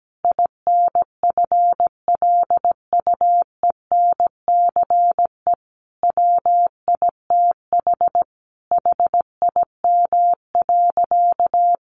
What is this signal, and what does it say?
Morse code, heard as a keyed tone, about 17 wpm, 700 Hz. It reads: INFLUENCE WITH HIM.
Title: Sentences from Top 1100 Words 17wpm